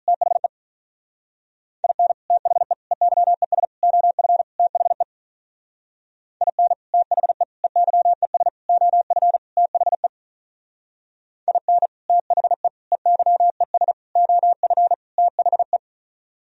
Morse code, heard as a keyed tone, about 42 wpm, 700 Hz. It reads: THE IN THE EYES OF THE IN THE EYES OF THE IN THE EYES OF THE
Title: Speed-Racing - Sets of 5 Words 35wpm